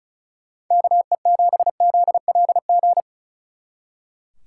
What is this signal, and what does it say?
Morse code, heard as a keyed tone, about 43 wpm, 700 Hz. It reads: KE7ZLG
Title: Speed Racing - USA Call Signs 35wpm